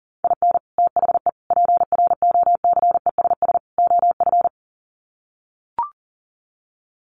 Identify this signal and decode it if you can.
Morse code, heard as a keyed tone, about 37 wpm, 700 Hz. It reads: IN THE PROCESS OF
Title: Sets of 4 Words 40wpm